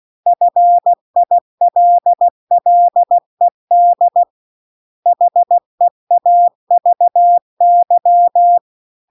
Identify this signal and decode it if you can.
Morse code, heard as a keyed tone, about 16 wpm, 700 Hz. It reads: FILLED HEAVY